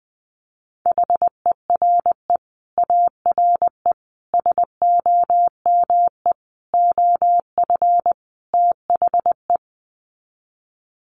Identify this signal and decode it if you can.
Morse code, heard as a keyed tone, about 20 wpm, 700 Hz. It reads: HERE ARE SOME OF THE
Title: Sets of 5 Words Encore 20wpm